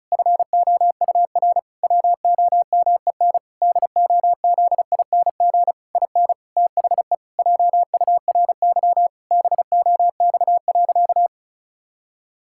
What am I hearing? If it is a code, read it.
Morse code, heard as a keyed tone, about 36 wpm, 700 Hz. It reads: FOUR WOMEN DOZING IN THE JURY BOX.
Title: Pangram Challenge v2 - 35wpm